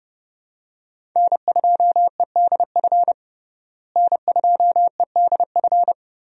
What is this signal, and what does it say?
Morse code, heard as a keyed tone, about 30 wpm, 700 Hz. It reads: N2EDF N2EDF
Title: USA Call Signs - 5 Characters - Repeated 3x - 30wpm